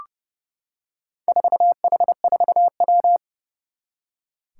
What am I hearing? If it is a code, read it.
Morse code, heard as a keyed tone, about 30 wpm, 690 Hz. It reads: 4H4W
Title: Four Challenging Letter-Number Combinations 30wpm